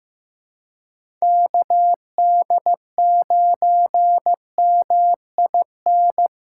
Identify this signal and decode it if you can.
Morse code, heard as a keyed tone, about 15 wpm, 700 Hz. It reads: KD9MIN
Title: USA Callsign DE Callsign 15wpm